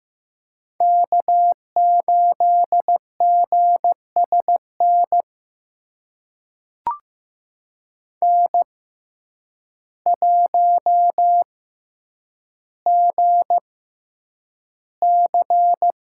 Morse code, heard as a keyed tone, about 15 wpm, spaced as for 8 wpm, 700 Hz. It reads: K8GSN N1GC
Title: USA Call Signs - 5 Characters - 6x Character Spacing - 15wpm